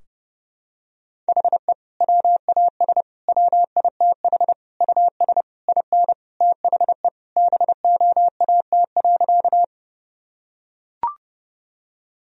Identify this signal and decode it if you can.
Morse code, heard as a keyed tone, about 32 wpm, 700 Hz. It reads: HE WAS WITH US IN THE BOAT.
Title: Sentences from Top 1000 Words - Review Part 3 - 30wpm